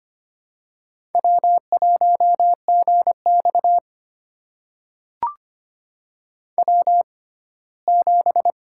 Morse code, heard as a keyed tone, about 25 wpm, spaced as for 17 wpm, 700 Hz. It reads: W1GX W7
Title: USA Call Signs - 4 Characters - 6x Character Spacing - 25wpm